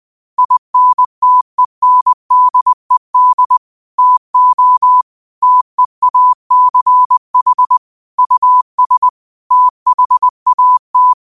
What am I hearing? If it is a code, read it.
Morse code, heard as a keyed tone, about 20 wpm, 1000 Hz. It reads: INTENDED TO TEACH US THAT
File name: qotdfiles / 20wpm